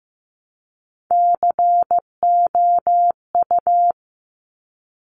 Morse code, heard as a keyed tone, about 15 wpm, 700 Hz. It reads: COU
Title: Top Three Letter Combinations in Words 15wpm